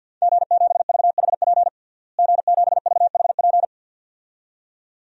Morse code, heard as a keyed tone, about 51 wpm, 700 Hz. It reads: G7VHP G7VHP
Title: DXCC-100 Call Signs - 5 Characters - Repeated 2x - 50wpm